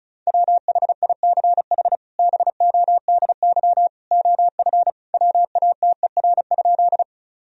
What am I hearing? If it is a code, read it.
Morse code, heard as a keyed tone, about 36 wpm, 700 Hz. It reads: WHICH BODY OF WATER?
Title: Sentences from Top 1000 Words - Review Part 2 - 35wpm